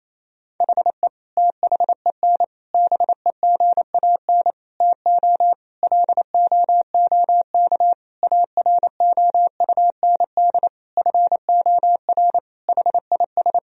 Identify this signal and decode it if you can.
Morse code, heard as a keyed tone, about 28 wpm, 700 Hz. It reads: HE THEN BEGAN TO LOOK AROUND FOR HIS
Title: Sentences from Top 1000 Words - Review Part 3 - 28wpm